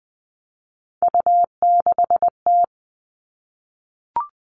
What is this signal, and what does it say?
Morse code, heard as a keyed tone, about 20 wpm, 700 Hz. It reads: U6T
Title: All Cut Numbers - 3-Digit 20wpm